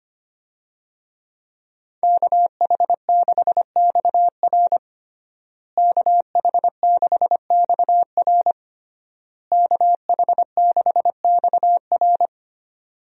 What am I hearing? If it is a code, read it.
Morse code, heard as a keyed tone, about 25 wpm, 700 Hz. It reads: KH6XR KH6XR KH6XR
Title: DXCC-100 Call Signs - 5 Characters - Repeated 3x - 25wpm